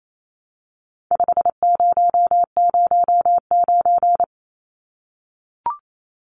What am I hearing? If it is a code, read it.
Morse code, heard as a keyed tone, about 28 wpm, 700 Hz. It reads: 5009
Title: Numbers - 4-Digit 28wpm